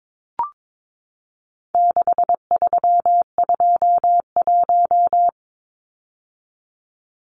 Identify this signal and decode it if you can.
Morse code, heard as a keyed tone, about 22 wpm, 700 Hz. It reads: 6321
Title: Numbers - 4-Digit 22wpm